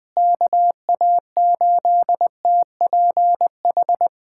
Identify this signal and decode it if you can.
Morse code, heard as a keyed tone, about 21 wpm, 700 Hz. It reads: KA8TPH
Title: USA Callsign DE Callsign - 6 Characters - 20wpm